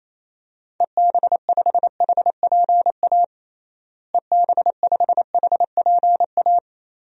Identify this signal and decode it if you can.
Morse code, heard as a keyed tone, about 28 wpm, 700 Hz. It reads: EB5HPA EB5HPA
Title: DXCC-100 Call Signs - 6 Characters - Repeated 3x - 28wpm